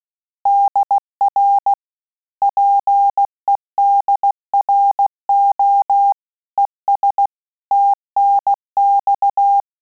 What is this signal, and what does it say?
Morse code, heard as a keyed tone, about 16 wpm, 800 Hz. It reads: DR PEDRO ES TNX
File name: Telegraphy-group / QSO CW 15 PPM